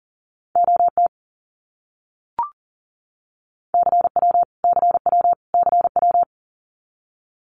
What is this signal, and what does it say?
Morse code, heard as a keyed tone, about 40 wpm, 700 Hz. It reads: OT CW CW CW
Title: Abbreviations Repeated 3x - 40wpm